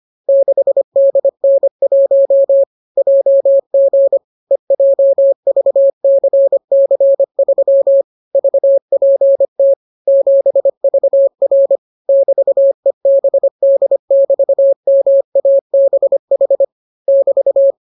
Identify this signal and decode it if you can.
Morse code, heard as a keyed tone, about 25 wpm, 550 Hz. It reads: 6DN1 JG EJVCC3 VPT 7VR <BT>EBD<BT>MABH <BT>